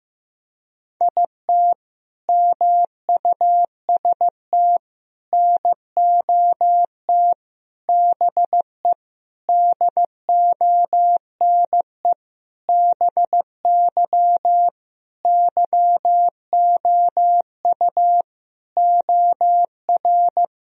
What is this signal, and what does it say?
Morse code, heard as a keyed tone, about 15 wpm, 700 Hz. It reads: IT MUST NOT BE DONE BY YOU OR
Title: Sentences from Top 1000 Words - Review Part 1 - 15wpm